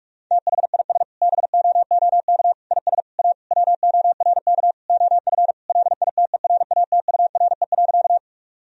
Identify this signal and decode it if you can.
Morse code, heard as a keyed tone, about 48 wpm, 700 Hz. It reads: THIS BOOK IS A WORK OF LITERATURE.
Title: Sentences from Top 1000 Words - Review Part 3 - 45wpm